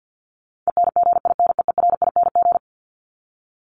Morse code, heard as a keyed tone, about 50 wpm, 700 Hz. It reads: ENGINEERING
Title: ING Suffix - Top 100 Words with Base Words 50wpm